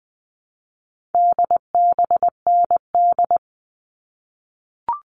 Morse code, heard as a keyed tone, about 20 wpm, 700 Hz. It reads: DBND
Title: All Cut Numbers - 4-Digit 20wpm